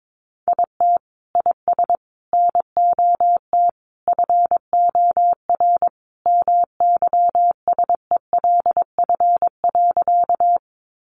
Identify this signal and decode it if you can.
Morse code, heard as a keyed tone, about 22 wpm, 700 Hz. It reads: IT IS NOT FOR MYSELF.
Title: Sentences from Top 1300 Words 22wpm